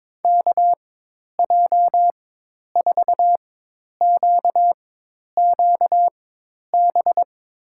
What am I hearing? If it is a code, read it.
Morse code, heard as a keyed tone, about 22 wpm, spaced as for 10 wpm, 700 Hz. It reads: KJ4QQB
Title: USA Call Signs - 6 Characters - 4x Character Spacing - 22wpm